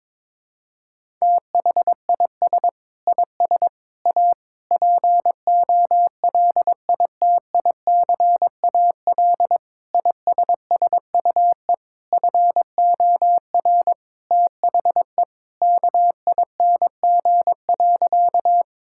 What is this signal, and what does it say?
Morse code, heard as a keyed tone, about 22 wpm, 700 Hz. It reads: THIS IS A POLITICAL ISSUE FOR THE KING.
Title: Sentences from Top 1000 Words - Review Part 6 - 22wpm